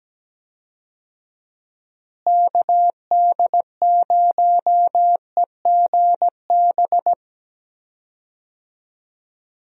Morse code, heard as a keyed tone, about 17 wpm, 700 Hz. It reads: KD0EGB
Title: USA Call Signs - 6 Characters - Repeated 2x - 17wpm